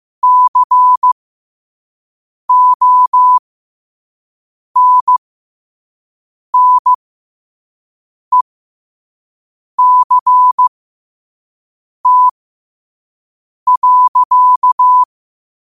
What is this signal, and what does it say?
Morse code, heard as a keyed tone, about 15 wpm, spaced as for 5 wpm, 1000 Hz. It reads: CONNECT.
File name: 5wpm